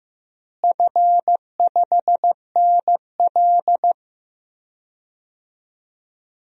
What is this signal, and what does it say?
Morse code, heard as a keyed tone, about 15 wpm, 700 Hz. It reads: F5NL
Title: DXCC-100 Call Signs - 4 Characters - Repeated 2x - 15wpm